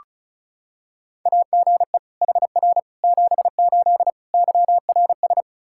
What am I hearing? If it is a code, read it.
Morse code, heard as a keyed tone, about 36 wpm, 700 Hz. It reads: AGE HR 78 YRS
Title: K1USN SST QSO Practice 35wpm